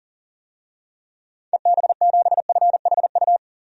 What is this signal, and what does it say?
Morse code, heard as a keyed tone, about 40 wpm, 700 Hz. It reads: EB7FHU
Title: DXCC-100 Call Signs - 6 Characters - 40wpm